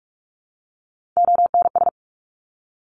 Morse code, heard as a keyed tone, about 46 wpm, 700 Hz. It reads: ONS
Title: ARRL Sections 45wpm